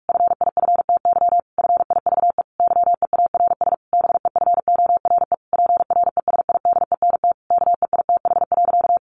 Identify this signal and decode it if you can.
Morse code, heard as a keyed tone, about 44 wpm, 700 Hz. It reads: FIFTY FIVE YEARS BEFORE PRESIDENT KEITH.
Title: A Sound Of Thunder 45wpm